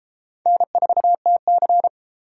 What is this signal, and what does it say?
Morse code, heard as a keyed tone, about 32 wpm, 700 Hz. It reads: N4TC